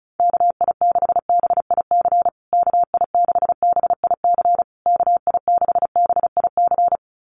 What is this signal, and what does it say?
Morse code, heard as a keyed tone, about 34 wpm, 700 Hz. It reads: KI6BIC KI6BIC KI6BIC
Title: USA Call Signs Repeated 3x 35wpm